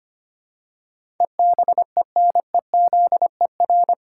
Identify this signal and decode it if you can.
Morse code, heard as a keyed tone, about 25 wpm, 700 Hz. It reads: EBENEZER
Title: Pangram Challenge v2 - 25wpm